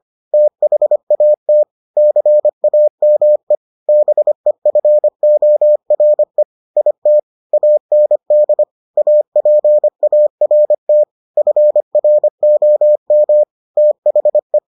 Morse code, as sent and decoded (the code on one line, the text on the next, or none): - .... .- - / -.-. .- -- . / -... ..-. --- .-. / .. - / .- -. -.. / .- .--. .- .-. - / ..-. .-. --- -- / - .... .
THAT CAME BEFORE IT AND APART FROM THE